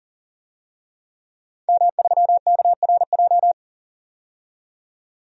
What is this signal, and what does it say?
Morse code, heard as a keyed tone, about 40 wpm, 700 Hz. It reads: M3KRJ
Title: DXCC-100 Call Signs - 5 Characters - 40wpm